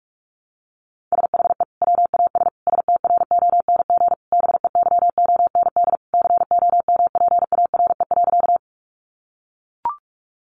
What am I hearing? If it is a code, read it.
Morse code, heard as a keyed tone, about 44 wpm, 700 Hz. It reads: SHE WAS STRONG BEYOND COMPARE.
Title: Sentences from Top 1300 Words 45wpm